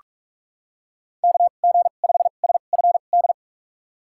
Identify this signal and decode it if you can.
Morse code, heard as a keyed tone, about 46 wpm, spaced as for 33 wpm, 700 Hz. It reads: KG5SFD
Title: USA Call Signs - 6 Characters - 2x Character Spacing - 45wpm